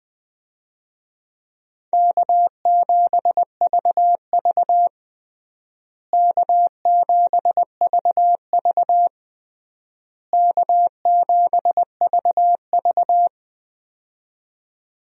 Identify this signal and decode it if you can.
Morse code, heard as a keyed tone, about 20 wpm, 700 Hz. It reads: K7VV K7VV K7VV
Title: USA Call Signs - 4 Characters - Repeated 3x - 20wpm